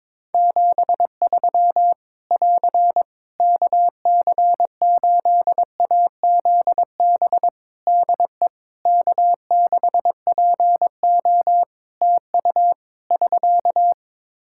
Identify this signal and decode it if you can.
Morse code, heard as a keyed tone, about 22 wpm, 700 Hz. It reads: 73 <AR> KC8AZB DE K6PO TU <SK>
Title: Practice QSO - Exchange 4 - Ending - 22wpm